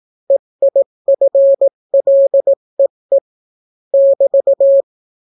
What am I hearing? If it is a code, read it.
Morse code, heard as a keyed tone, about 18 wpm, 550 Hz. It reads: EIFLEE <BT>